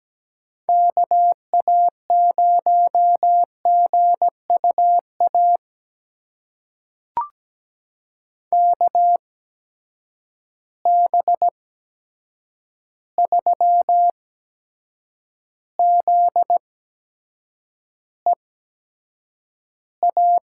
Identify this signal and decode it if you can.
Morse code, heard as a keyed tone, about 17 wpm, spaced as for 7 wpm, 700 Hz. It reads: KA0GUA KB3ZEA
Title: USA Call Signs - 6 Characters - 8x Character Spacing - 17wpm